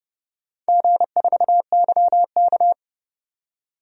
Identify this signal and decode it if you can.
Morse code, heard as a keyed tone, about 32 wpm, 700 Hz. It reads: G4YK
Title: DXCC-100 Call Signs - 4 Characters - 30wpm